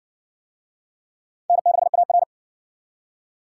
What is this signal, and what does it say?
Morse code, heard as a keyed tone, about 54 wpm, 700 Hz. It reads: N6AR